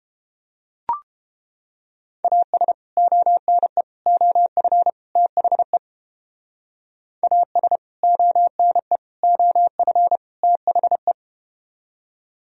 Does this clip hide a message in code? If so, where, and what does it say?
Yes: Morse code from 2.2 to 11.1 s, heard as a keyed tone, about 32 wpm, 700 Hz. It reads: AS ONE OF THE AS ONE OF THE